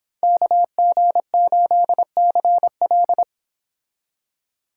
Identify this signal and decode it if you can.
Morse code, heard as a keyed tone, about 25 wpm, 700 Hz. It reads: KG8CL